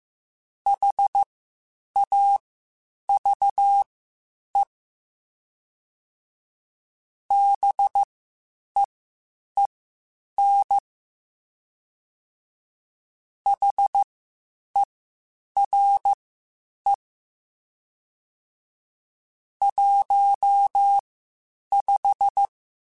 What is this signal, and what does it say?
Morse code, heard as a keyed tone, about 15 wpm, spaced as for 8 wpm, 800 Hz. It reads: HAVE BEEN HERE 15
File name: (spaced) / QSOs Spaced-mp3s / WD0BGP 1 @13 sp